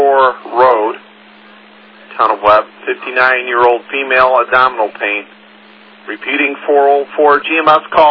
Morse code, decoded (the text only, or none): none